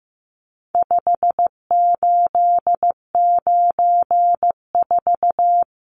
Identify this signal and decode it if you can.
Morse code, heard as a keyed tone, about 15 wpm, 700 Hz. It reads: 5894
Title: Numbers - 4-Digit 15wpm